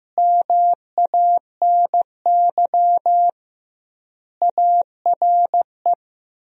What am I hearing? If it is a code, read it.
Morse code, heard as a keyed tone, about 15 wpm, 700 Hz. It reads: MANY ARE
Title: 2x Word-Spacing - Sentences from Top 100 Words - 15wpm